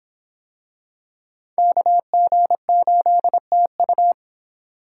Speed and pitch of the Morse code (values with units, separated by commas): 25 wpm, 700 Hz